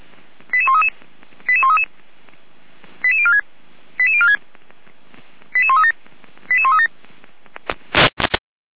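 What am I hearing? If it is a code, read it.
Five-tone selective call (ZVEI), heard as quick runs of five tones, tones of about 70 ms. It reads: 88133 88133 88046 88046 88137 88137
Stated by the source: ZVEI five-tone call received by SDR.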